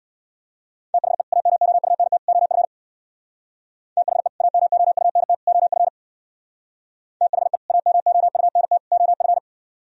Morse code, heard as a keyed tone, about 64 wpm, 700 Hz. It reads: THE AMOUNT OF THE AMOUNT OF THE AMOUNT OF